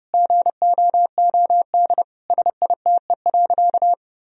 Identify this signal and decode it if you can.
Morse code, heard as a keyed tone, about 32 wpm, 700 Hz. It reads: GOOD SITE.
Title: Sentences from Top 1000 Words - Review Part 4 - 30wpm